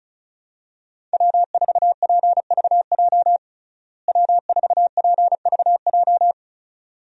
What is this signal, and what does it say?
Morse code, heard as a keyed tone, about 35 wpm, 700 Hz. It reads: W4PVJ W4PVJ
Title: USA Call Signs - 5 Characters - Repeated 2x - 35wpm